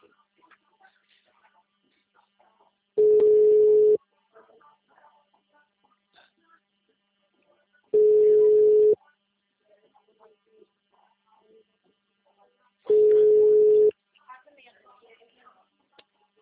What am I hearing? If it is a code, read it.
Ringback tone (Europe, 425 Hz), with speech.